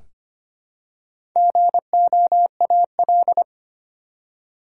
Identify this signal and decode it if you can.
Morse code, heard as a keyed tone, about 25 wpm, 700 Hz. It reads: GOAL
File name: practice / 25wpm / Easy Words with Hints - 25wpm